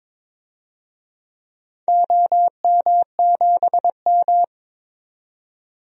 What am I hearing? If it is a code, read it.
Morse code, heard as a keyed tone, about 22 wpm, 700 Hz. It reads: OM7M
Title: DXCC-100 Call Signs - 4 Characters - 22wpm